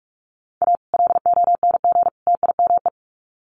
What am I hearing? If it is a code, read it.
Morse code, heard as a keyed tone, about 44 wpm, 700 Hz. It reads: A LONG TIME
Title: Sets of 3 Words 45wpm